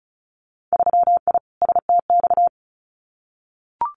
Morse code, heard as a keyed tone, about 36 wpm, 700 Hz. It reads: 3I STX
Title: Winter Field Day Exchanges Encore 35wpm